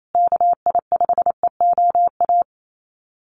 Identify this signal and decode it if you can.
Morse code, heard as a keyed tone, about 28 wpm, 700 Hz. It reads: KI5EOA
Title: USA Call Signs Repeated 3x 28wpm